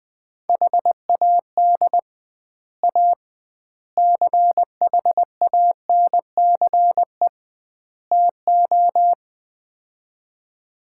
Morse code, heard as a keyed tone, about 20 wpm, 700 Hz. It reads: HAD A CHANCE TO
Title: 2x Word-Spacing - Sets of 4 Words - 20wpm